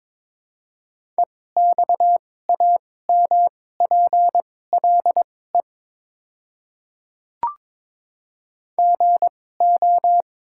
Morse code, heard as a keyed tone, about 22 wpm, spaced as for 16 wpm, 700 Hz. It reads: EXAMPLE GO
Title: 2x Character Spacing - Top 300 Words - 22wpm